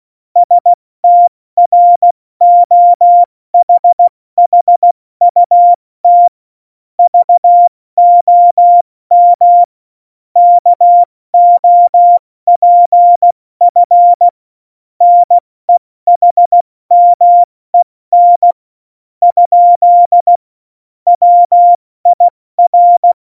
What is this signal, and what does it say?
Morse code, heard as a keyed tone, about 16 wpm, 700 Hz. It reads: STROHHUT VOM KOPF NEHMEN ? WIR
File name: Morsekurs / Witze / Witz 19. Juni 2018, 70 F80